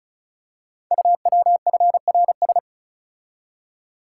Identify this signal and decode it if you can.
Morse code, heard as a keyed tone, about 36 wpm, 700 Hz. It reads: UWFRS
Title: Code Groups - 5 Letters - 35wpm